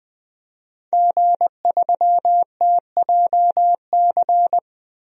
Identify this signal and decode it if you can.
Morse code, heard as a keyed tone, about 20 wpm, 700 Hz. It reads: G3TJC
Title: DXCC-100 Call Signs - 5 Characters - 20wpm